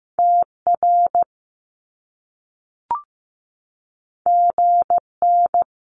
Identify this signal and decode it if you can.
Morse code, heard as a keyed tone, about 15 wpm, 700 Hz. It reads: TR GN
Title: Top Two Letter Combinations in Words 15wpm